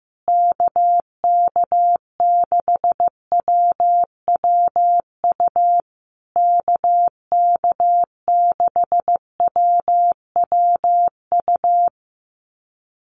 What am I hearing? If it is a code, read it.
Morse code, heard as a keyed tone, about 15 wpm, 700 Hz. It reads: KK6WWU KK6WWU
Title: USA Call Signs Repeated 3x 15wpm